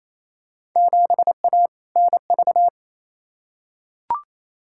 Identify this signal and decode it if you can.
Morse code, heard as a keyed tone, about 28 wpm, 700 Hz. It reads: 7A NV
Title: Field Day Exchanges Encore - 28wpm